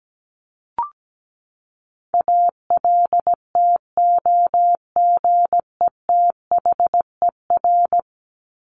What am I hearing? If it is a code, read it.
Morse code, heard as a keyed tone, about 17 wpm, 700 Hz. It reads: ALTOGETHER
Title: Top 1501-1750 Words 17wpm